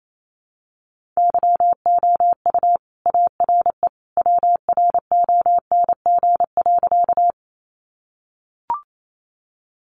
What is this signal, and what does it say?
Morse code, heard as a keyed tone, about 28 wpm, 700 Hz. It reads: YOU ARE WRONG.
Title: Sentences from Top 900 Words 28wpm